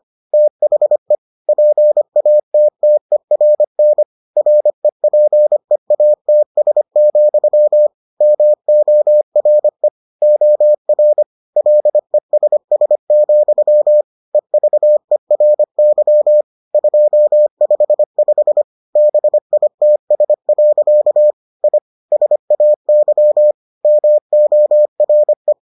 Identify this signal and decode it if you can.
Morse code, heard as a keyed tone, about 25 wpm, 600 Hz. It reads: THE PATTERN REPEATS, MORE OR LESS, EVERY 255 BITS. I SAY MORE